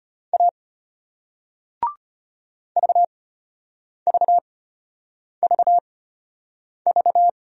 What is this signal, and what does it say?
Morse code, heard as a keyed tone, about 34 wpm, spaced as for 8 wpm, 700 Hz. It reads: A VVVV